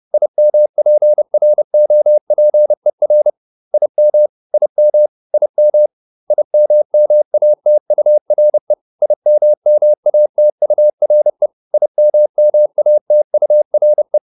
Morse code, heard as a keyed tone, about 30 wpm, 600 Hz. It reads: IMPROPER IM IM IM IMMATURE IMMATURE IMMATURE
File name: IM8 30